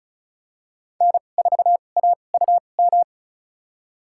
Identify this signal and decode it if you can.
Morse code, heard as a keyed tone, about 34 wpm, spaced as for 25 wpm, 700 Hz. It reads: N4AUM
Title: USA Call Signs - 5 Characters - 2x Character Spacing - 35wpm